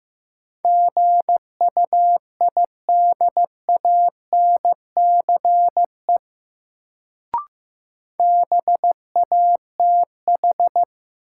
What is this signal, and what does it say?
Morse code, heard as a keyed tone, about 15 wpm, 700 Hz. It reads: GUIDANCE BATH
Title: Sentences from Top 1600 Words - New Word Drill Alt - 15wpm